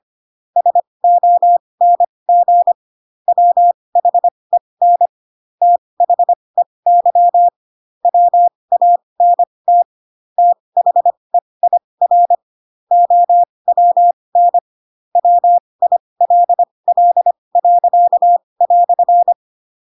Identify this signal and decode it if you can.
Morse code, heard as a keyed tone, about 25 wpm, spaced as for 20 wpm, 700 Hz. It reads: SONG WHEN THEY WANT THEIR OWN WILL."
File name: Pinocchio / Pinocchio ch8 15 25 ik1hge